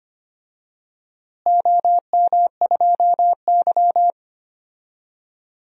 Morse code, heard as a keyed tone, about 25 wpm, 700 Hz. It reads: OM2Y
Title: DXCC-100 Call Signs - 4 Characters - Repeated 3x - 25wpm